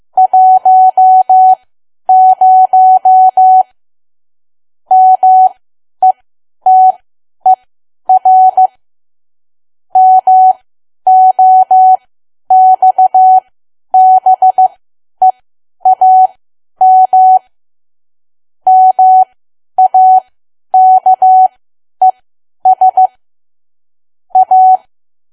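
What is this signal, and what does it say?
Morse code, heard as a keyed tone, about 15 wpm, spaced as for 10 wpm, 750 Hz. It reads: 10 METER MOXBEAM MAKES A
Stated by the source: Morse learning practice at 10 wpm.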